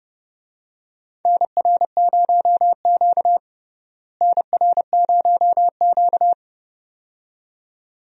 Morse code, heard as a keyed tone, about 30 wpm, 700 Hz. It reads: NR0Q NR0Q
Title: USA Call Signs - 4 Characters - Repeated 2x - 30wpm